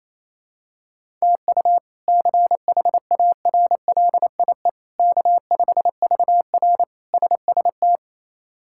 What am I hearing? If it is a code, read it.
Morse code, heard as a keyed tone, about 28 wpm, 700 Hz. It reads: TU CHARLIE K5VR SST
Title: K1USN SST QSO Practice - 28wpm